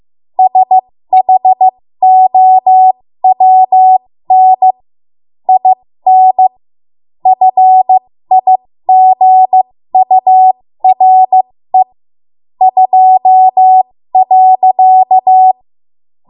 Morse code, heard as a keyed tone, about 15 wpm, spaced as for 13 wpm, 750 Hz. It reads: SHOWN IN FIGURE 2.
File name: Morse / 13WPM